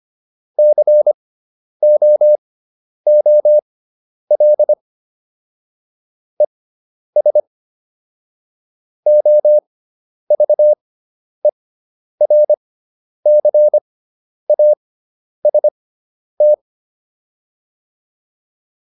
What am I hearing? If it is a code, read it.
Morse code, heard as a keyed tone, about 25 wpm, spaced as for 10 wpm, 600 Hz. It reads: COOL ES OVERCAST